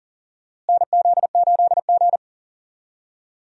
Morse code, heard as a keyed tone, about 40 wpm, 700 Hz. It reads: NZ8G
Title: USA Call Signs - 4 Characters - Repeated 2x - 40wpm